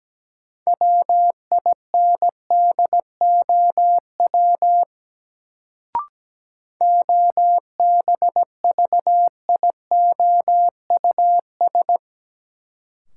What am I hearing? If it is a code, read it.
Morse code, heard as a keyed tone, about 17 wpm, 700 Hz. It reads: WINDOW OBVIOUS
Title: Sentences from Top 1600 Words - New Word Drill Alt - 17wpm